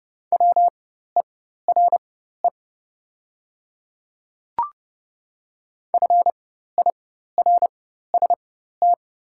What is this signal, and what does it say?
Morse code, heard as a keyed tone, about 30 wpm, spaced as for 14 wpm, 700 Hz. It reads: WERE FIRST
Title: 4x Character Spacing - Top 500 Words - 30wpm